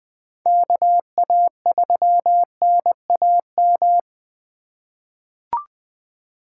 Morse code, heard as a keyed tone, about 20 wpm, 700 Hz. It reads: KA3NAM
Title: K1USN SST QSO Practice - 20wpm